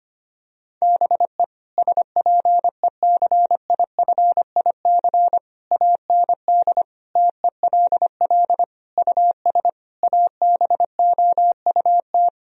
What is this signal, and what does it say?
Morse code, heard as a keyed tone, about 25 wpm, 700 Hz. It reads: BE SPECIFIC AND TELL US ABOUT
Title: Sentences from Top 1000 Words - Review Part 3 - 25wpm